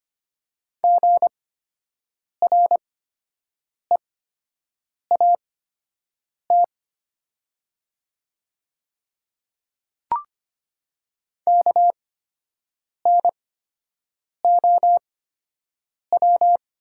Morse code, heard as a keyed tone, about 25 wpm, spaced as for 7 wpm, 700 Hz. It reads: GREAT KNOW
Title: 8x Character Spacing - Top 500 Words - 25wpm